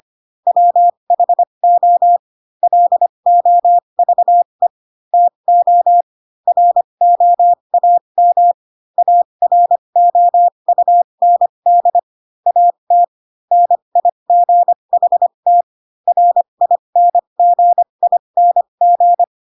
Morse code, heard as a keyed tone, about 25 wpm, spaced as for 22 wpm, 700 Hz. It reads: WHO LOVE TO ROAM AROUND AT NIGHT RINGING
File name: Pinocchio / Pinocchio ch6 18 25 ik1hge